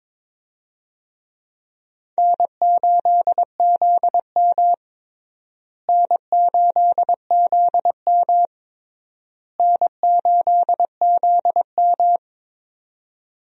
Morse code, heard as a keyed tone, about 22 wpm, 700 Hz. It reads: N8ZM N8ZM N8ZM
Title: USA Call Signs - 4 Characters - Repeated 3x - 22wpm